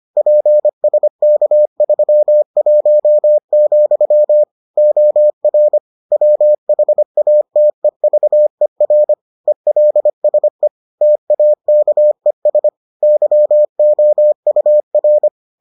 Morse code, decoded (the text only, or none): PSK31, OR WHATEVER ELSE TAKES YOUR